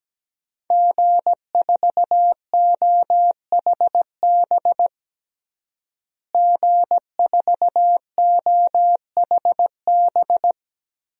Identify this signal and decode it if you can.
Morse code, heard as a keyed tone, about 17 wpm, 700 Hz. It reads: G4OHB G4OHB
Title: DXCC-100 Call Signs - 5 Characters - Repeated 2x - 17wpm